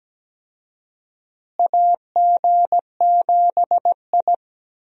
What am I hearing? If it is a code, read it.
Morse code, heard as a keyed tone, about 17 wpm, 700 Hz. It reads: AG7I
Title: USA Call Signs - 4 Characters - Repeated 3x - 17wpm